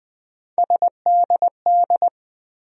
Morse code, heard as a keyed tone, about 21 wpm, 700 Hz. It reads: SDD
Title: USA Call Sign Postfixes - 3 Letters 20wpm